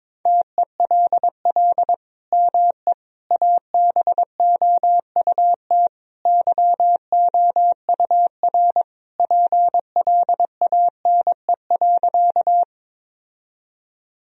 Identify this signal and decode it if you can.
Morse code, heard as a keyed tone, about 22 wpm, 700 Hz. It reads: TELL ME ABOUT YOUR PLANE.
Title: Sentences from Top 1000 Words - Review Part 3 - 22wpm